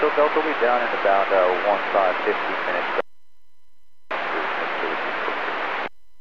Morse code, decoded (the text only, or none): none